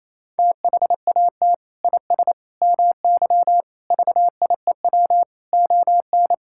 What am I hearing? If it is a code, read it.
Morse code, heard as a keyed tone, about 28 wpm, 700 Hz. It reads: THAT IS MY VIEW ON
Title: Sentences from Top 1000 Words - Review Part 6 - 28wpm